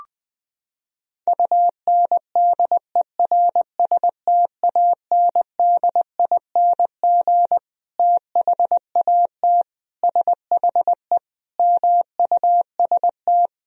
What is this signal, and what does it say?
Morse code, heard as a keyed tone, about 21 wpm, 700 Hz. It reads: UNDERSTANDING THAT SHE MUST
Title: Sentences from Top 1000 Words - Review Part 3 - 20wpm